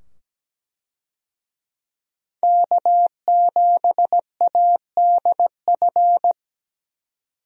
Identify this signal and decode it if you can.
Morse code, heard as a keyed tone, about 17 wpm, 700 Hz. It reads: K7ADF
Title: USA Call Signs - 5 Characters - 17wpm